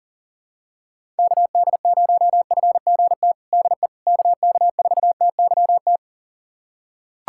Morse code, heard as a keyed tone, about 40 wpm, 700 Hz. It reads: KD0FGT DE KK4TYT
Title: USA Callsign DE Callsign - 6 Characters - 40wpm